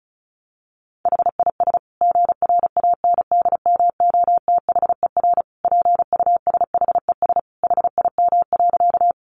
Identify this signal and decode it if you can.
Morse code, heard as a keyed tone, about 34 wpm, 700 Hz. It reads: HIS GRANDMOTHER PUSHES HIM.
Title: Additional Easy Phrases 35wpm